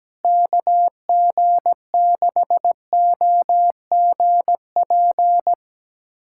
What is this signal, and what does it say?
Morse code, heard as a keyed tone, about 17 wpm, 700 Hz. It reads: KG6OGP
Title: USA Callsign DE Callsign - 6 Characters - 17wpm